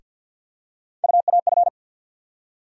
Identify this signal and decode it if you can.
Morse code, heard as a keyed tone, about 52 wpm, 690 Hz. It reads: UAF
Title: USA Call Sign Postfixes - 3 Letters 50wpm